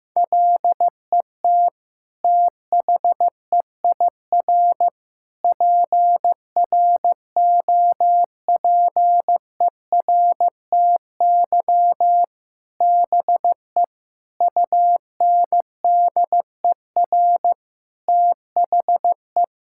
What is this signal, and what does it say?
Morse code, heard as a keyed tone, about 15 wpm, 700 Hz. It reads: LET THEIR PROPERTY BE UNDER THE
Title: Sentences from Top 1600 Words 15wpm